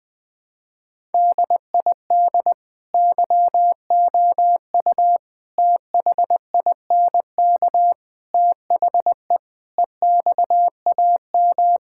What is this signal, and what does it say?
Morse code, heard as a keyed tone, about 20 wpm, 700 Hz. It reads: DID YOU THINK THE EXAM
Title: Sentences from Top 1000 Words - Review Part 5 - 20wpm